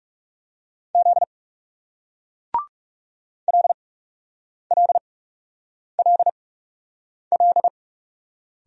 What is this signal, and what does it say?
Morse code, heard as a keyed tone, about 43 wpm, spaced as for 8 wpm, 700 Hz. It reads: Z LLLL